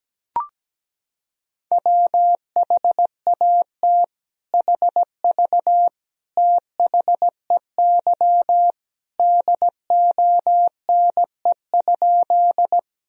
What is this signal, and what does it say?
Morse code, heard as a keyed tone, about 17 wpm, 700 Hz. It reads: WHAT HV THEY DONE?
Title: Sentences from Top 500 Words and Abbreviations 17wpm